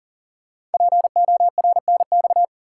Morse code, heard as a keyed tone, about 39 wpm, 690 Hz. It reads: PORNX
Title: Code Groups - 5 Letters - 40wpm